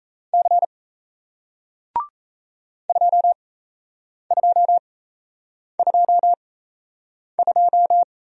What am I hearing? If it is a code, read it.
Morse code, heard as a keyed tone, about 38 wpm, spaced as for 8 wpm, 700 Hz. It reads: C 2222